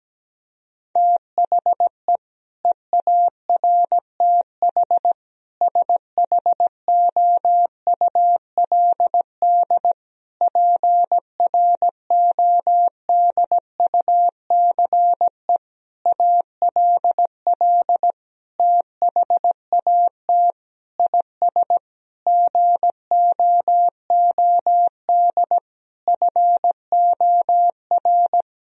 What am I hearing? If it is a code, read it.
Morse code, heard as a keyed tone, about 17 wpm, 700 Hz. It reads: THE EARTH SHOULD PRODUCE ALL THAT IS GOOD FOR
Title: Sentences from Top 1000 Words - Review Part 3 - 17wpm